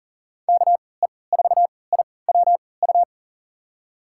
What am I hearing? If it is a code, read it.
Morse code, heard as a keyed tone, about 40 wpm, spaced as for 23 wpm, 700 Hz. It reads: KE4IWU
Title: USA Call Signs - 6 Characters - 3x Character Spacing - 40wpm